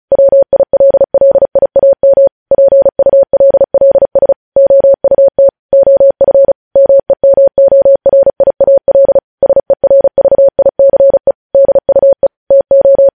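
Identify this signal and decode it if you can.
Morse code, heard as a keyed tone, about 36 wpm, 550 Hz. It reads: WILLIAM PULLS OUT OF MEMORIAL SERVICE DUE TO
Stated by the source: Morse code news headlines at 35.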